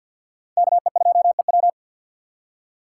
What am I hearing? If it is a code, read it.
Morse code, heard as a keyed tone, about 53 wpm, 700 Hz. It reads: KE2EW
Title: USA Call Signs - 5 Characters - 3x Character Spacing - 50wpm